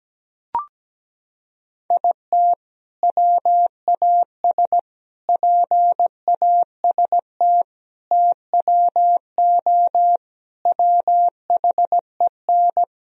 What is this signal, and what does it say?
Morse code, heard as a keyed tone, about 17 wpm, 700 Hz. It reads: IT WAS PAST TWO WHEN
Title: Sentences from Top 1000 Words - Review Part 3 - 17wpm